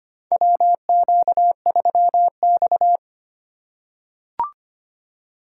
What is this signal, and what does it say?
Morse code, heard as a keyed tone, about 25 wpm, 700 Hz. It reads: WQ3X
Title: USA Call Signs - 4 Characters - 2x Character Spacing - 25wpm